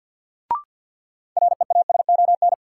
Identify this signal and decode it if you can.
Morse code, heard as a keyed tone, about 51 wpm, 700 Hz. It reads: REASON
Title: Sets of 5 Words Encore - New Word Drill - 50wpm